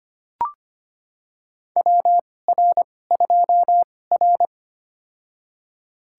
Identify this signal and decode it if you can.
Morse code, heard as a keyed tone, about 25 wpm, spaced as for 18 wpm, 700 Hz. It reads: WR2R
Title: USA Call Signs - 4 Characters - 2x Character Spacing - 25wpm